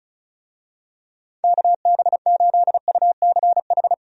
Morse code, heard as a keyed tone, about 36 wpm, 700 Hz. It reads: KB8UCH
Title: USA Call Signs - 6 Characters - 35wpm